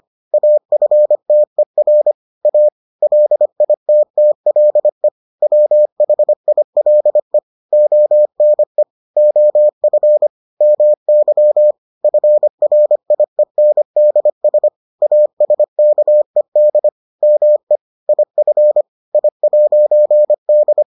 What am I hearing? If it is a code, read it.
Morse code, heard as a keyed tone, about 25 wpm, 600 Hz. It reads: AFTER A LITTLE WHILE ONE OF MY FRIENDS ASKED ME IF I'D